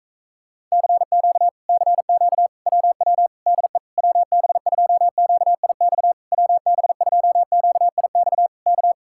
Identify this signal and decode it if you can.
Morse code, heard as a keyed tone, about 44 wpm, 700 Hz. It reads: CQ CQ WW DE WB2QIX WB2QIX K